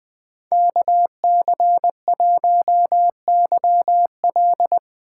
Morse code, heard as a keyed tone, about 20 wpm, 700 Hz. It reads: KC1YL
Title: 2023 POTA Activator Call Signs - 20wpm